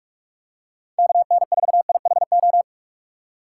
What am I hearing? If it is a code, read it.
Morse code, heard as a keyed tone, about 46 wpm, 700 Hz. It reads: KN4IHO
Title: USA Call Signs - 6 Characters - 6x Character Spacing - 45wpm